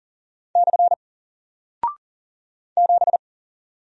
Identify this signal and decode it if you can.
Morse code, heard as a keyed tone, about 39 wpm, 700 Hz. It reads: / 7
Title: Single Character (Letter:Number:Punctuation:Prosign) 40wpm